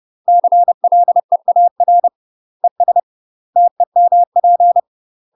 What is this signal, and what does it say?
Morse code, heard as a keyed tone, about 30 wpm, 700 Hz. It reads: CLEAR ES TEMP
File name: UK MASTER QSOs TRAINING / 30WPM / UK MASTER QSOs 0021